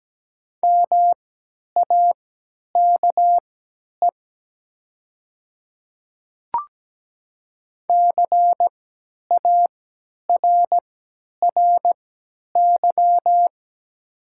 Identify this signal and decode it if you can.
Morse code, heard as a keyed tone, about 17 wpm, spaced as for 10 wpm, 700 Hz. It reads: MAKE CARRY
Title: 3x Character Spacing - Top 500 Words - 17wpm